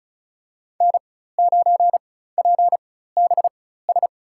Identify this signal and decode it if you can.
Morse code, heard as a keyed tone, about 34 wpm, spaced as for 16 wpm, 700 Hz. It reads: N9PBS
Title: USA Call Signs - 5 Characters - 4x Character Spacing - 35wpm